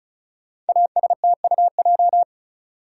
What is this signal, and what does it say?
Morse code, heard as a keyed tone, about 35 wpm, 700 Hz. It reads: ASTUJ
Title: Code Groups - 5 Letters - 35wpm